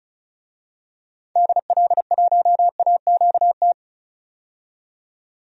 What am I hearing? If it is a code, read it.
Morse code, heard as a keyed tone, about 36 wpm, 700 Hz. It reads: DL1AQT
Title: DXCC-100 Call Signs - 6 Characters - 35wpm